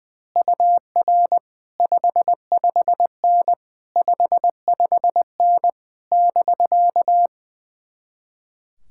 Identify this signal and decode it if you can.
Morse code, heard as a keyed tone, about 20 wpm, 700 Hz. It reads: UR 55N 55N <BK>
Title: Speed-Racing Common QSO Phrases - 20wpm